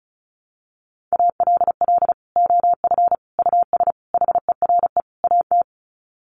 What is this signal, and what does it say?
Morse code, heard as a keyed tone, about 34 wpm, 700 Hz. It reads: ALL OF US HERE AT
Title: Sets of 5 Words 35wpm